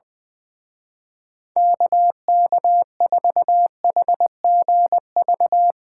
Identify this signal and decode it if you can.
Morse code, heard as a keyed tone, about 22 wpm, 700 Hz. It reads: KK4HGV
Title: Speed Racing - USA 2x3 Call Signs 20wpm